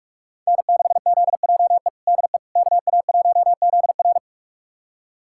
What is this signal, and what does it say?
Morse code, heard as a keyed tone, about 45 wpm, 700 Hz. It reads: N6ZJE DE KA1ZR
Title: USA Callsign DE Callsign - 5 Characters - 45wpm